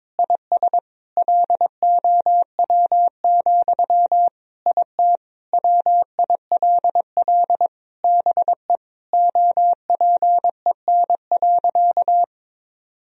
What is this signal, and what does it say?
Morse code, heard as a keyed tone, about 22 wpm, 700 Hz. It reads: IS LOW, IT WILL BE OPEN.
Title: Sentences from Top 1000 Words - Review Part 1 - 22wpm